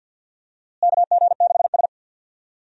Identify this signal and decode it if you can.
Morse code, heard as a keyed tone, about 48 wpm, 700 Hz. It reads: KG6S
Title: USA Call Signs - 4 Characters - Repeated 3x - 50wpm